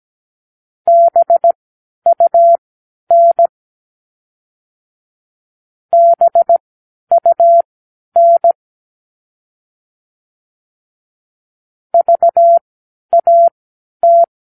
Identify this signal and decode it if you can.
Morse code, heard as a keyed tone, about 17 wpm, spaced as for 10 wpm, 680 Hz. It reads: BUN BUN VAT